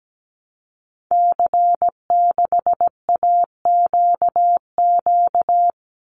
Morse code, heard as a keyed tone, about 17 wpm, 700 Hz. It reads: C6AQQ
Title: Call Signs Repeated 2x - 17wpm